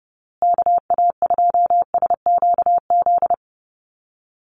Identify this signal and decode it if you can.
Morse code, heard as a keyed tone, about 30 wpm, 700 Hz. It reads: KA2SQZ
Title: USA Call Signs Repeated 2x 30wpm